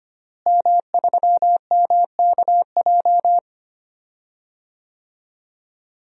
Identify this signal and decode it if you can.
Morse code, heard as a keyed tone, about 25 wpm, 700 Hz. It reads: M3MKJ
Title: DXCC-100 Call Signs - 5 Characters - Repeated 3x - 25wpm